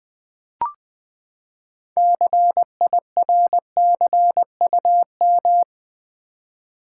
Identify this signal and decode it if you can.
Morse code, heard as a keyed tone, about 20 wpm, 700 Hz. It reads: CIRCUM
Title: Build A Word - 6 Letters At A Time - 12-Letter Words - 20wpm